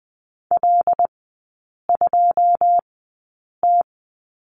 Morse code, heard as a keyed tone, about 20 wpm, spaced as for 8 wpm, 700 Hz. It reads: L2T
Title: 107 Lesson 18e - Letter m - Three Characters 20wpm